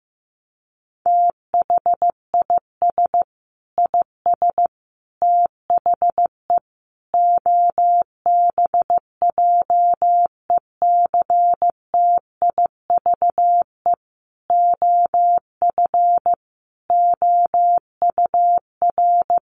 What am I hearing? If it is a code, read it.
Morse code, heard as a keyed tone, about 15 wpm, 700 Hz. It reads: THIS IS THE OBJECTIVE OF OUR
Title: Sentences from Top 1200 Words 15wpm